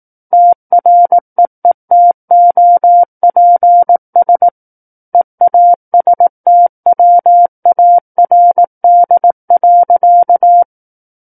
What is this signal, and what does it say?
Morse code, heard as a keyed tone, about 18 wpm, 700 Hz. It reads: TREETOPS EASTWARD.
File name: 18 WPM